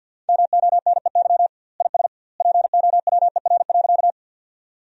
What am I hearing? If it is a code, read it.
Morse code, heard as a keyed tone, about 50 wpm, 700 Hz. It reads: MONEY IS POWER.
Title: 2x Word-Spacing - Sentences from Top 500 Words - 50wpm